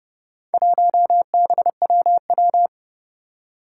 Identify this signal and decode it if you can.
Morse code, heard as a keyed tone, about 30 wpm, 700 Hz. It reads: 1BWW
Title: Four Challenging Letter-Number Combinations 30wpm